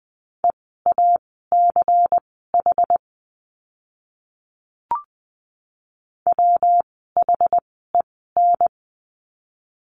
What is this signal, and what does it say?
Morse code, heard as a keyed tone, about 20 wpm, spaced as for 15 wpm, 700 Hz. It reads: EACH WHEN
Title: 192 Lesson 31f - Letter v - Words 20wpm